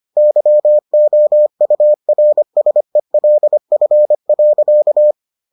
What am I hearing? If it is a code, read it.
Morse code, heard as a keyed tone, about 25 wpm, 600 Hz. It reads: YOURSELF.